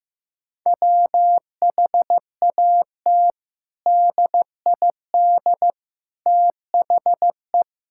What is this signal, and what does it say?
Morse code, heard as a keyed tone, about 15 wpm, 700 Hz. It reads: WHAT DID THE
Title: Sentences from Top 1000 Words - Review Part 5 - 15wpm